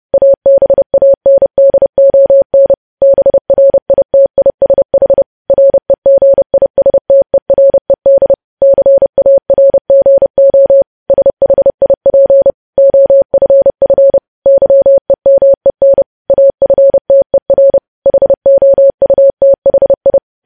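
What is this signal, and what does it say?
Morse code, heard as a keyed tone, about 30 wpm, 550 Hz. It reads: ABANDON BRITISH REGISTERED CARGO SHIP OFF YEMEN AFTER HOUTHI
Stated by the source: Morse code news headlines sent at 30.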